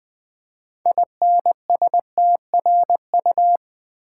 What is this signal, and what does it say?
Morse code, heard as a keyed tone, about 20 wpm, 700 Hz. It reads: INSTRU
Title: Build A Word - 6 Letters At A Time - 12-Letter Words - 20wpm